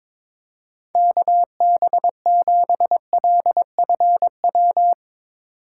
Morse code, heard as a keyed tone, about 24 wpm, 700 Hz. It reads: KB7LFW
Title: Speed Racing - USA 2x3 Call Signs 22wpm